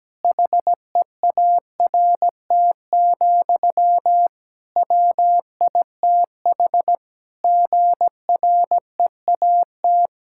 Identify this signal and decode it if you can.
Morse code, heard as a keyed tone, about 17 wpm, 700 Hz. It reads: HEART, WITH GREAT
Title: Sentences from Top 1600 Words 17wpm